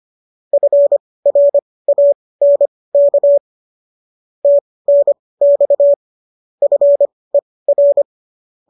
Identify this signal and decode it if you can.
Morse code, heard as a keyed tone, about 25 wpm, spaced as for 18 wpm, 590 Hz. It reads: FRANK TNX FER